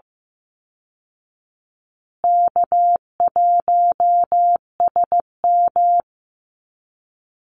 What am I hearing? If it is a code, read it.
Morse code, heard as a keyed tone, about 15 wpm, 700 Hz. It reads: K1SM
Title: Easy Call Signs Repeated 3x - 15wpm